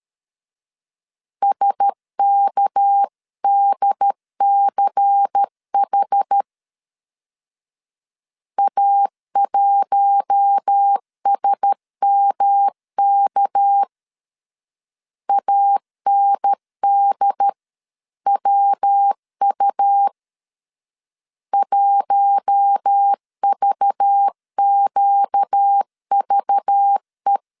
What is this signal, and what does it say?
Morse code, heard as a keyed tone, about 13 wpm, 800 Hz. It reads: SKDCH A1SMK ANDWU 1VQVE
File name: CW / DET / 1 6 1500C S A 12wpm 3440